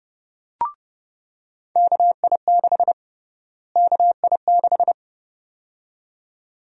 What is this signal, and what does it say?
Morse code, heard as a keyed tone, about 30 wpm, 700 Hz. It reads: KI6 KI6
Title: USA Call Sign Prefixes - Repeated 2x - 30wpm